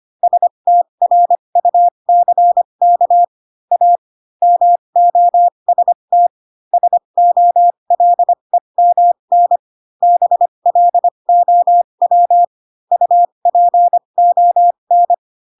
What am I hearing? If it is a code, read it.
Morse code, heard as a keyed tone, about 25 wpm, spaced as for 22 wpm, 700 Hz. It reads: STRUCK A MOST SOLEMN BLOW UPON